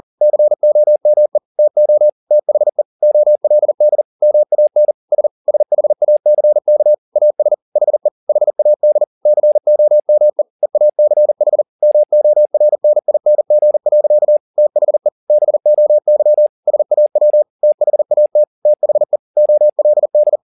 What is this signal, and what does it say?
Morse code, heard as a keyed tone, about 40 wpm, 600 Hz. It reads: COME TO THE OLD MAN S SHACK AS HE HAD COME EACH MORNING. THE BOY SAW THAT THE OLD
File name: the old man and the sea 2 40wpm